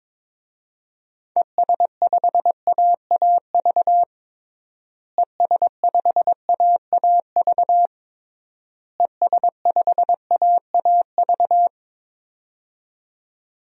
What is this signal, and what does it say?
Morse code, heard as a keyed tone, about 22 wpm, 700 Hz. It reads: ES5AAV ES5AAV ES5AAV
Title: DXCC-100 Call Signs - 6 Characters - Repeated 3x - 22wpm